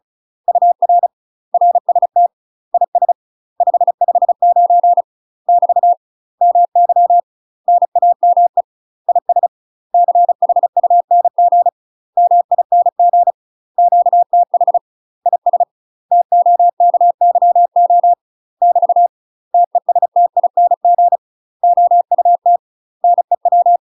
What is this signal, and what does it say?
Morse code, heard as a keyed tone, about 36 wpm, 700 Hz. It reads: UR RST IS 559 <BT> MY NAME IS CHUNG MING QTH IS TOKYO <BT> TESTING OUT NEW
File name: UK MASTER QSOs TRAINING / 35WPM / UK MASTER QSOs 0026